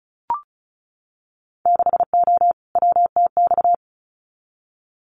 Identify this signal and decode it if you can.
Morse code, heard as a keyed tone, about 34 wpm, 700 Hz. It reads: 6O WTX
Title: Winter Field Day Exchanges Encore 35wpm